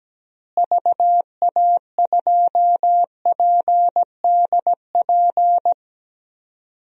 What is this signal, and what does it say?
Morse code, heard as a keyed tone, about 17 wpm, 700 Hz. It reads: VA2PDP
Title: DXCC-100 Call Signs - 6 Characters - Repeated 2x - 17wpm